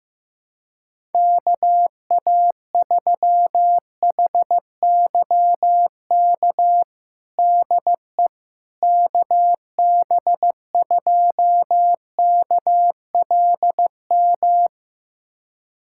Morse code, heard as a keyed tone, about 15 wpm, 700 Hz. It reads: KA3HYK DE KB2KLM
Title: USA Callsign DE Callsign - 6 Characters - 15wpm